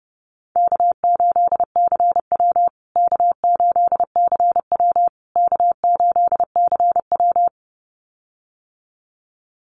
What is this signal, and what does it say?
Morse code, heard as a keyed tone, about 30 wpm, 700 Hz. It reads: K8CW K8CW K8CW
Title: USA Call Signs Repeated 3x 30wpm